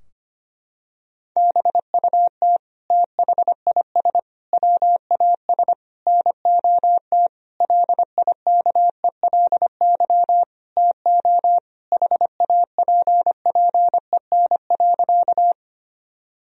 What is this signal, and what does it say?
Morse code, heard as a keyed tone, about 25 wpm, 700 Hz. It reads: BUT THIS WAS NOT LIKELY TO HAPPEN.
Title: Sentences from Top 1000 Words - Review Part 4 - 25wpm